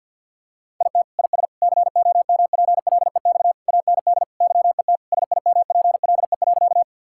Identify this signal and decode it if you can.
Morse code, heard as a keyed tone, about 50 wpm, 700 Hz. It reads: IT IS COMPLEX AND YET SIMPLE.
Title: Sentences from Top 1000 Words - Review Part 6 - 50wpm